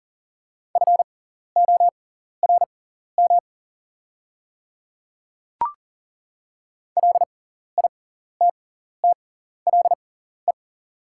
Morse code, heard as a keyed tone, about 40 wpm, spaced as for 14 wpm, 700 Hz. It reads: FORM LITTLE
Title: 6x Character Spacing - Top 500 Words - 40wpm